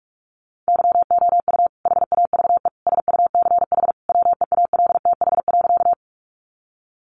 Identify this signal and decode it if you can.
Morse code, heard as a keyed tone, about 44 wpm, 700 Hz. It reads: YOU HAVE SUCH WEALTH.
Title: Sentences from Top 1200 Words 45wpm